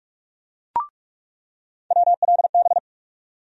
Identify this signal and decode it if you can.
Morse code, heard as a keyed tone, about 46 wpm, 700 Hz. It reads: WLB
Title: USA Call Sign Postfixes - 3 Letters 45wpm